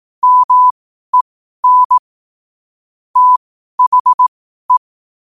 Morse code, heard as a keyed tone, about 18 wpm, spaced as for 12 wpm, 1000 Hz. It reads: MEN THE
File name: qotdfiles / 12wpm